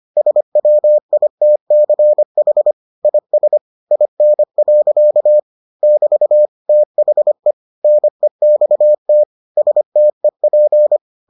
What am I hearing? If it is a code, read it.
Morse code, heard as a keyed tone, about 25 wpm, 600 Hz. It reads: SWITCH IS IN. <BT> THE NEXT STEP